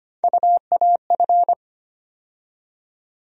Morse code, heard as a keyed tone, about 25 wpm, 700 Hz. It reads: UAF
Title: USA Call Sign Postfixes - 3 Letters 25wpm